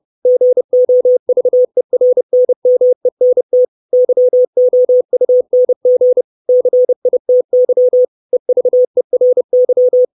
Morse code, heard as a keyed tone, about 30 wpm, 500 Hz. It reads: GOVERNMENT YOUNG CITY EVERY
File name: 632 Words 30WPM